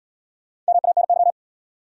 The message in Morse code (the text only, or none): MANY